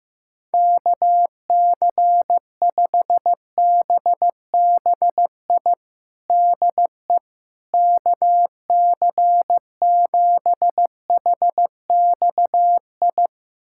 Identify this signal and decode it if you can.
Morse code, heard as a keyed tone, about 16 wpm, 700 Hz. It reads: KC5BBI DE KC7HXI
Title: USA Callsign DE Callsign - 6 Characters - 15wpm